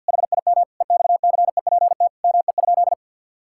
Morse code, heard as a keyed tone, about 50 wpm, 700 Hz. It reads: HIM EXCEPT ME?
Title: Sentences from Top 1600 Words 50wpm